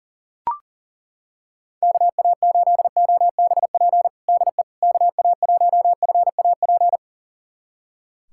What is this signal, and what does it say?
Morse code, heard as a keyed tone, about 40 wpm, 700 Hz. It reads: KA8OBP DE KA1FAP
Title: USA Callsign DE Callsign - 6 Characters - 40wpm